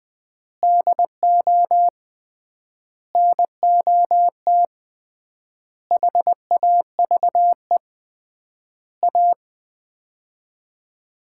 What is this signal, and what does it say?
Morse code, heard as a keyed tone, about 21 wpm, 700 Hz. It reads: DO NOT HAVE A
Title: 3x Word-Spacing - Sets of 5 Words - 20wpm